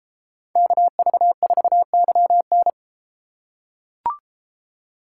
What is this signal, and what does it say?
Morse code, heard as a keyed tone, about 31 wpm, 700 Hz. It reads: KV4YN